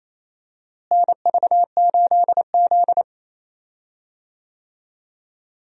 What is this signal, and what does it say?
Morse code, heard as a keyed tone, about 28 wpm, 700 Hz. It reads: NV8Z
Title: USA Call Signs - 4 Characters - Repeated 3x - 28wpm